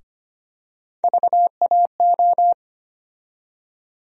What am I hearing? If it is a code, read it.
Morse code, heard as a keyed tone, about 25 wpm, 700 Hz. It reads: VAO
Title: USA Call Sign Postfixes - 3 Letters 25wpm